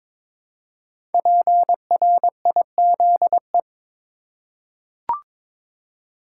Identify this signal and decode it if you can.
Morse code, heard as a keyed tone, about 22 wpm, 700 Hz. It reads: PRIZE
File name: 5 Letter Words - First 3 Letters Given - 22wpm